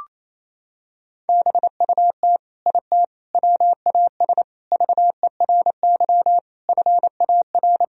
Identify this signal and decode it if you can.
Morse code, heard as a keyed tone, about 28 wpm, 700 Hz. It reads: BUT IT WAS VERY FAR
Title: Sentences from Top 1000 Words - Review Part 2 - 28wpm